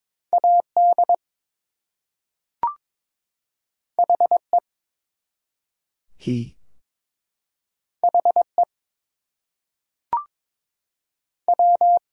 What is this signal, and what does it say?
Morse code, heard as a keyed tone, about 22 wpm, 700 Hz. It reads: AD HE HE W
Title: Two Letter Words - 22wpm